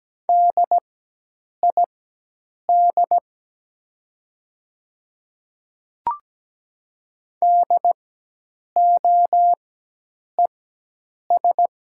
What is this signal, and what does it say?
Morse code, heard as a keyed tone, about 17 wpm, spaced as for 8 wpm, 700 Hz. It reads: DID DOES
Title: 4x Character Spacing - Top 500 Words - 17wpm